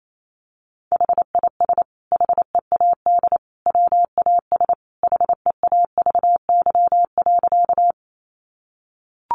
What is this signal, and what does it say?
Morse code, heard as a keyed tone, about 28 wpm, 700 Hz. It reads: HIS HEAD WAS HEAVY.
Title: Sentences from Top 1100 Words 28wpm